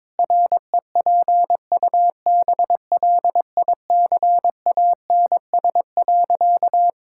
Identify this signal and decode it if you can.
Morse code, heard as a keyed tone, about 22 wpm, 700 Hz. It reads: REPUBLICANS.
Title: Pangram Challenge v2 - 22wpm